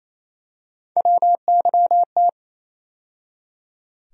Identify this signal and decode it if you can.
Morse code, heard as a keyed tone, about 28 wpm, 700 Hz. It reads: WYT
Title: USA Call Sign Postfixes - 3 Letters 28wpm